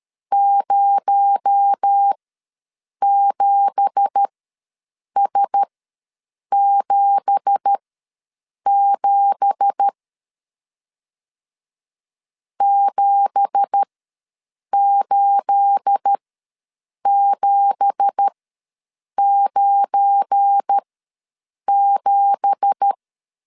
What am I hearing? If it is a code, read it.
Morse code, heard as a keyed tone, about 13 wpm, spaced as for 7 wpm, 800 Hz. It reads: 07S77 78797
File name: LEZIONI / 9 2867 ws 12wpm 3431